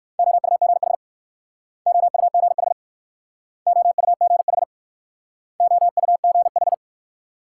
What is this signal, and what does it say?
Morse code, heard as a keyed tone, about 60 wpm, 700 Hz. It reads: OUGH OUGH OUGH OUGH